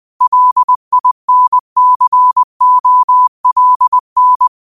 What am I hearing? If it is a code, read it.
Morse code, heard as a keyed tone, about 20 wpm, 1000 Hz. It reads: LINCOLN